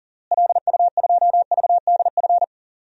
Morse code, heard as a keyed tone, about 37 wpm, 700 Hz. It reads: LU2VDF
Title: DXCC-100 Call Signs - 6 Characters - Repeated 3x - 40wpm